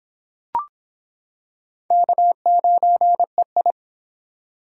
Morse code, heard as a keyed tone, about 25 wpm, 700 Hz. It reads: K9EI